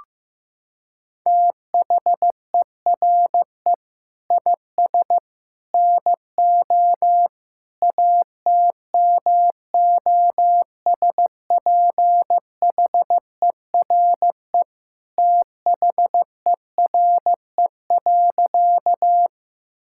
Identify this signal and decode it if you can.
Morse code, heard as a keyed tone, about 15 wpm, 700 Hz. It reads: THERE IS NO ATMOSPHERE THERE.
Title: Sentences from Top 1600 Words 15wpm